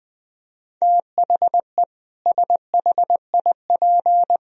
Morse code, heard as a keyed tone, about 20 wpm, 700 Hz. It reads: THE SHIP
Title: Sentences from Top 1000 Words - Review Part 4 - 20wpm